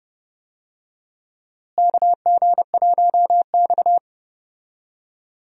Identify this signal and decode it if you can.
Morse code, heard as a keyed tone, about 32 wpm, 700 Hz. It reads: KG1X
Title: USA Call Signs - 4 Characters - Repeated 2x - 30wpm